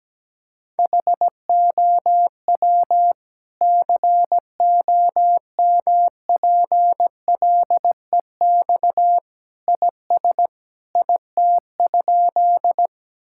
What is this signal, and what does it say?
Morse code, heard as a keyed tone, about 17 wpm, 700 Hz. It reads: HOW COMPLEX IS IT?
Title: Sentences from Top 1000 Words - Review Part 6 - 17wpm